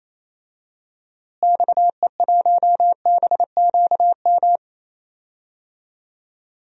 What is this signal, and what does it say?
Morse code, heard as a keyed tone, about 27 wpm, 700 Hz. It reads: XE1BQM
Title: DXCC-100 Call Signs - 6 Characters - Repeated 2x - 28wpm